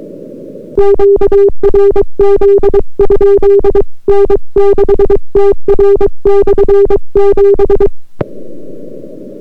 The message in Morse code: QRZ? N6TR/7